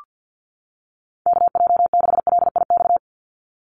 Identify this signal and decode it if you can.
Morse code, heard as a keyed tone, about 48 wpm, 700 Hz. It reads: KJ6LIX
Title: USA Call Signs 50wpm